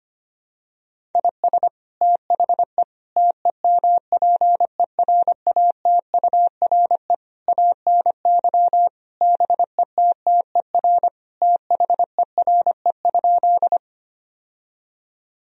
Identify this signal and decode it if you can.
Morse code, heard as a keyed tone, about 25 wpm, 700 Hz. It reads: IS THE TEMPERATURE ANY BETTER THERE?
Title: Sentences from Top 1000 Words - Review Part 4 - 25wpm